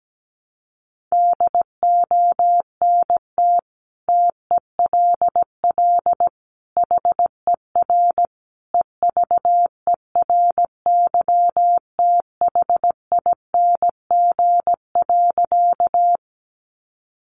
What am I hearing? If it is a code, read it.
Morse code, heard as a keyed tone, about 17 wpm, 700 Hz. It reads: DONT TELL HER EVERYTHING.
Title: Sentences from Top 700 Words 17wpm